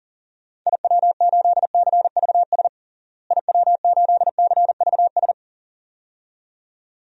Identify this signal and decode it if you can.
Morse code, heard as a keyed tone, about 37 wpm, 700 Hz. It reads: IW8CVS IW8CVS
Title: DXCC-100 Call Signs - 6 Characters - Repeated 2x - 40wpm